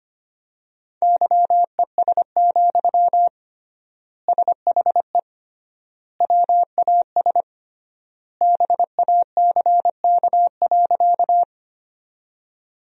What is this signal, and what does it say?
Morse code, heard as a keyed tone, about 25 wpm, 700 Hz. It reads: YES, SHE WAS BACK.
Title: 3x Word-Spacing - Sentences from Top 500 Words - 25wpm